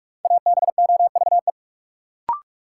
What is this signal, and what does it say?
Morse code, heard as a keyed tone, about 42 wpm, 700 Hz. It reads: ABOVE
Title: Sentences from Top 1000 Words - Review Part 2 - New Word Drill - 45wpm